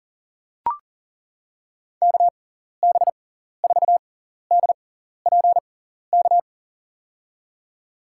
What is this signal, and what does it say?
Morse code, heard as a keyed tone, about 39 wpm, spaced as for 14 wpm, 700 Hz. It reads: KB4DPK
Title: USA Call Signs - 6 Characters - 6x Character Spacing - 40wpm